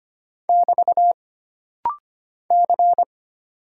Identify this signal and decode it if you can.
Morse code, heard as a keyed tone, about 25 wpm, 700 Hz. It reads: <BT> C
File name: Single Character (Letter:Number:Punctuation:Prosign) - Rapid-Fire - 25wpm